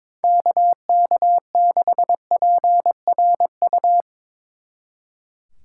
Morse code, heard as a keyed tone, about 24 wpm, 700 Hz. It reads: KK6PRU
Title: Speed Racing - USA 2x3 Call Signs 22wpm